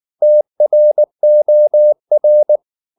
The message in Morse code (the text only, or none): TROR